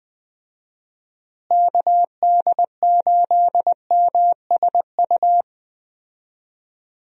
Morse code, heard as a keyed tone, about 21 wpm, 700 Hz. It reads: KD8MSU
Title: USA Call Signs - 6 Characters - Repeated 3x - 20wpm